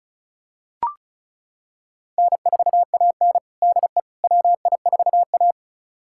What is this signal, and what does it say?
Morse code, heard as a keyed tone, about 34 wpm, 700 Hz. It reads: N4AN DE WI4A
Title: USA Callsign DE Callsign - 4 Characters - 35wpm